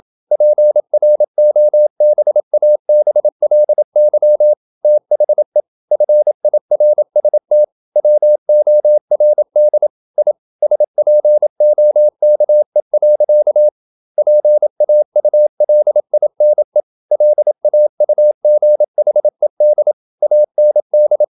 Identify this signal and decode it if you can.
Morse code, heard as a keyed tone, about 28 wpm, 600 Hz. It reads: PROBABLY THE FIRST WORD I SPOKE. PAULINE LAUGHED AND
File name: Never 0027wpm